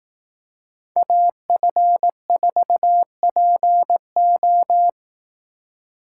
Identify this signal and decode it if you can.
Morse code, heard as a keyed tone, about 18 wpm, 700 Hz. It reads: AF4PO
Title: Speed Racing - USA Call Signs 15wpm